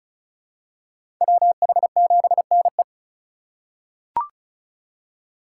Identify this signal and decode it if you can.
Morse code, heard as a keyed tone, about 36 wpm, 700 Hz. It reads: WH7NE
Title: USA Call Signs - 4x Character Spacing - 35wpm